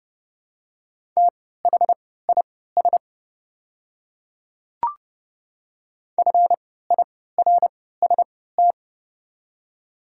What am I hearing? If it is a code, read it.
Morse code, heard as a keyed tone, about 29 wpm, spaced as for 17 wpm, 690 Hz. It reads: THIS FIRST
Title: 3x Character Spacing - Top 100 Words - 30wpm